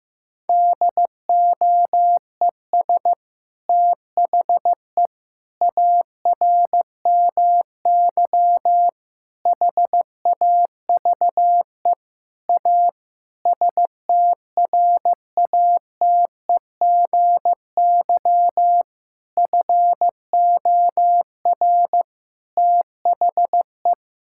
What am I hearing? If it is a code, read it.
Morse code, heard as a keyed tone, about 15 wpm, 700 Hz. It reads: DOES THE ARMY HAVE A STRATEGY FOR THE
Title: Sentences from Top 1000 Words - Review Part 5 - 15wpm